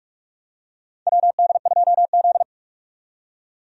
Morse code, heard as a keyed tone, about 47 wpm, 700 Hz. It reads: WD2Z
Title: USA Call Signs - 4 Characters - 45wpm